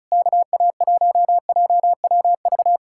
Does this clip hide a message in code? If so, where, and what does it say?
Yes: Morse code from 0.1 to 2.8 s, heard as a keyed tone, about 40 wpm, 700 Hz. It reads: KA1JWV